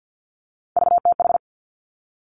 Morse code, heard as a keyed tone, about 50 wpm, 700 Hz. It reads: VTH
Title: Three Challenging Letter-Number Combinations 50wpm